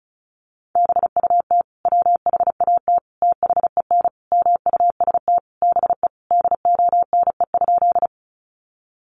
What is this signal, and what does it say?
Morse code, heard as a keyed tone, about 34 wpm, 700 Hz. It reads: BUT WHAT THEN MUST BE DONE?
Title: Sentences from Top 500 Words 35wpm